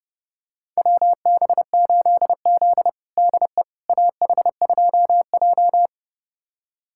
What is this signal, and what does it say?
Morse code, heard as a keyed tone, about 30 wpm, 700 Hz. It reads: WB8Z DE AH2J
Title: USA Callsign DE Callsign - 4 Characters - 30wpm